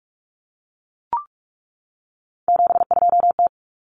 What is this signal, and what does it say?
Morse code, heard as a keyed tone, about 44 wpm, 700 Hz. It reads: 72T
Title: Common Cut Numbers - 3-Digit 45wpm